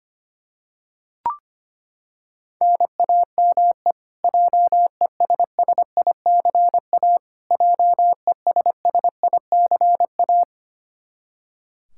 Morse code, heard as a keyed tone, about 25 wpm, 700 Hz. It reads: NAME JESSICA JESSICA
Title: Practice QSO - Exchange 1 - Beginning - 25wpm